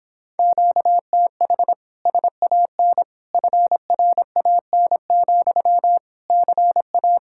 Morse code, heard as a keyed tone, about 26 wpm, 700 Hz. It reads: QTH SAN FRAN, CA